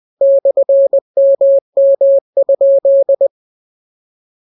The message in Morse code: /MM?